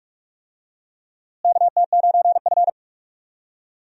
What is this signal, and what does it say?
Morse code, heard as a keyed tone, about 46 wpm, 700 Hz. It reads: KT9F
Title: USA Call Signs - 4 Characters - Repeated 2x - 45wpm